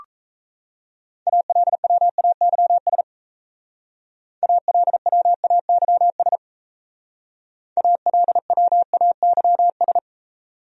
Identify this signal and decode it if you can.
Morse code, heard as a keyed tone, about 36 wpm, 700 Hz. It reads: ALWAYS ALWAYS ALWAYS